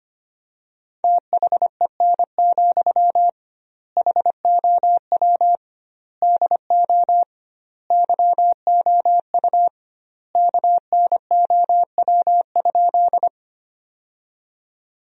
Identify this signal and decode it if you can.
Morse code, heard as a keyed tone, about 25 wpm, 700 Hz. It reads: THEN, HOW DO YOU KNOW?
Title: 2x Word-Spacing - Sentences from Top 100 Words - 25wpm